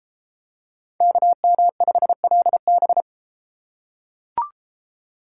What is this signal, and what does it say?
Morse code, heard as a keyed tone, about 30 wpm, 700 Hz. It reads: KM5LB